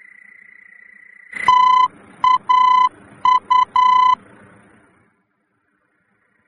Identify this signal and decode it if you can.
Morse code, heard as a keyed tone, about 10 wpm, 1010 Hz. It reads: TAU
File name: germany / mx / TAU-D-VOR